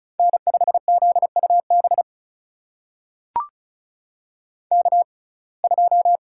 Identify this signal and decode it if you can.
Morse code, heard as a keyed tone, about 36 wpm, spaced as for 25 wpm, 700 Hz. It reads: N5ZUB K2
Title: USA Call Signs - 6x Character Spacing - 35wpm